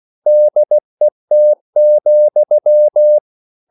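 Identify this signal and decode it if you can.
Morse code, heard as a keyed tone, about 16 wpm, 600 Hz. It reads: DET,
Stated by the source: Morse code at a slow 15 wpm from CW med Gnister.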